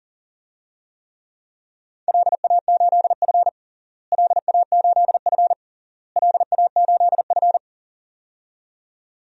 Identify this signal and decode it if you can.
Morse code, heard as a keyed tone, about 40 wpm, 700 Hz. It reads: LA8F LA8F LA8F
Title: DXCC-100 Call Signs - 4 Characters - Repeated 3x - 40wpm